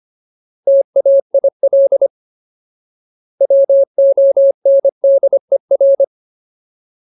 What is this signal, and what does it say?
Morse code, heard as a keyed tone, about 25 wpm, 550 Hz. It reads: TAIL WONDER
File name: Ultimate Word List / 25WPM Ultimate